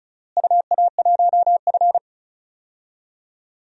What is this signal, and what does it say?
Morse code, heard as a keyed tone, about 36 wpm, 690 Hz. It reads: UA1F
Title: DXCC-100 Call Signs - 4 Characters - Repeated 3x - 35wpm